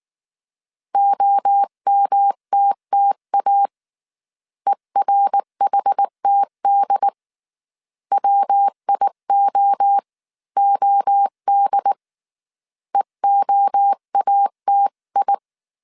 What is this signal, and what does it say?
Morse code, heard as a keyed tone, about 19 wpm, 800 Hz. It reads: OMTTA ERHTD WIOOD EOATI